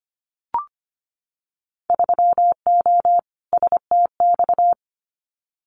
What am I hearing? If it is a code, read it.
Morse code, heard as a keyed tone, about 25 wpm, 700 Hz. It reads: 3O STX
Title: Winter Field Day Exchanges 25wpm